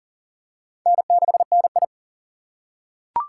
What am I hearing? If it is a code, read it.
Morse code, heard as a keyed tone, about 40 wpm, 700 Hz. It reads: N6NI
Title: USA Call Signs - 4 Characters - 3x Character Spacing - 40wpm